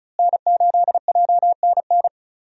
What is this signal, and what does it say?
Morse code, heard as a keyed tone, about 34 wpm, 700 Hz. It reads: N8JNN
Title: USA Call Signs - 5 Characters - Repeated 2x - 35wpm